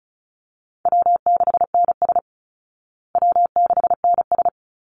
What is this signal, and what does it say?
Morse code, heard as a keyed tone, about 34 wpm, 700 Hz. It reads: W6NS W6NS
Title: Easy Call Signs Repeated 2x - 35wpm